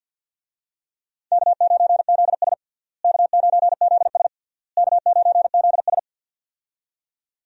Morse code, heard as a keyed tone, about 46 wpm, 700 Hz. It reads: K9ZS K9ZS K9ZS
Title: USA Call Signs - 4 Characters - Repeated 3x - 50wpm